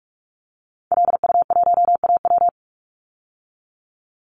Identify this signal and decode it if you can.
Morse code, heard as a keyed tone, about 46 wpm, 700 Hz. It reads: LU1AW
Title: Call Signs - 45wpm